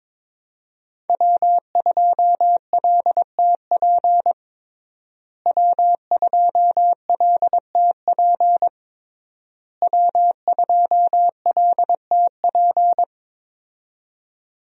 Morse code, heard as a keyed tone, about 22 wpm, 700 Hz. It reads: W2LTP W2LTP W2LTP
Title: USA Call Signs - 5 Characters - Repeated 3x - 22wpm